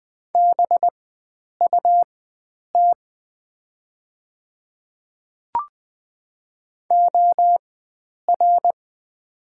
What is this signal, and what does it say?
Morse code, heard as a keyed tone, about 20 wpm, spaced as for 9 wpm, 700 Hz. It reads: BUT OR
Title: 4x Character Spacing - Top 100 Words - 20wpm